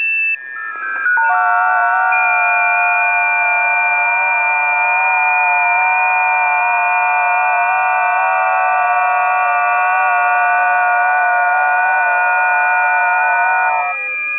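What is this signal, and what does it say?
FT8.
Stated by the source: FT8 signals recorded on a busy 20 m band.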